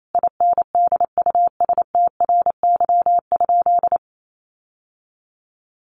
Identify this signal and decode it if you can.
Morse code, heard as a keyed tone, about 28 wpm, 700 Hz. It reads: INDUSTRY?
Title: Sentences from Top 1300 Words 28wpm